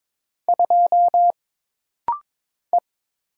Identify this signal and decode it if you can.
Morse code, heard as a keyed tone, about 22 wpm, 700 Hz. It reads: 2 E
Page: This rapid-fire timing is geared to push your ICR (Instant Character Recognition) skills.